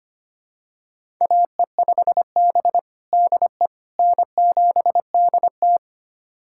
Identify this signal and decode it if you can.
Morse code, heard as a keyed tone, about 25 wpm, 700 Hz. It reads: AE5B DE N7DT
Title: USA Callsign DE Callsign - 4 Characters - 25wpm